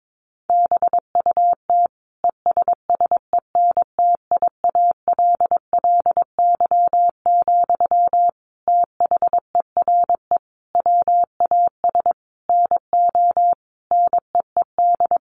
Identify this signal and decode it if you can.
Morse code, heard as a keyed tone, about 22 wpm, 700 Hz. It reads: BUT ESSENTIALLY, THERE WAS NO NEED
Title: Sentences from Top 1300 Words 22wpm